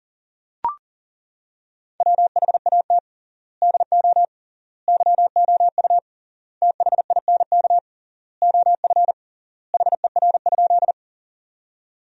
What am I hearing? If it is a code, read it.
Morse code, heard as a keyed tone, about 37 wpm, 700 Hz. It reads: WHAT DO YOU THINK OF HER?
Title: 3x Word-Spacing - Sentences from Top 200 Words - 40wpm